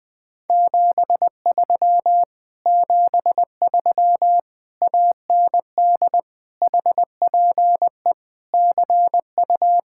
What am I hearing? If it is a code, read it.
Morse code, heard as a keyed tone, about 20 wpm, 700 Hz. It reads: 73 73 AND HPE CU
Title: Speed-Racing Common QSO Phrases - 17wpm